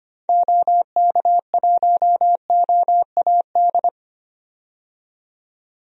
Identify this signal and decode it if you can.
Morse code, heard as a keyed tone, about 25 wpm, 700 Hz. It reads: OK1OAD
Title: DXCC-100 Call Signs - 6 Characters - 25wpm